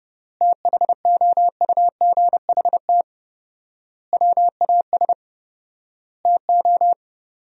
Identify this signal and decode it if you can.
Morse code, heard as a keyed tone, about 32 wpm, 700 Hz. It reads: THOUGHT WAS TO
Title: 4x Word-Spacing - Sentences from Top 400 Words - 30wpm